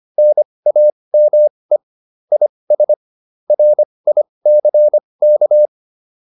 Morse code, heard as a keyed tone, about 25 wpm, spaced as for 20 wpm, 600 Hz. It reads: NAME IS RICK
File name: PR301 20